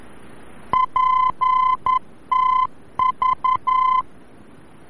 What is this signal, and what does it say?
Morse code, heard as a keyed tone, about 11 wpm, 1020 Hz. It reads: PTV
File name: france / mx / PTV-VOR-F